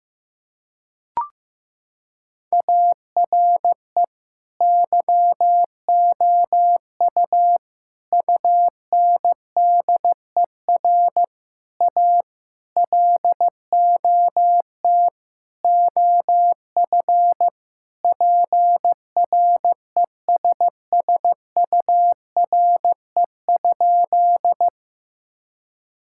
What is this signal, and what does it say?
Morse code, heard as a keyed tone, about 16 wpm, 700 Hz. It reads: ARE YOU UNDER A LOT OF PRESSURE?
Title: Sentences from Top 1000 Words - Review Part 6 - 15wpm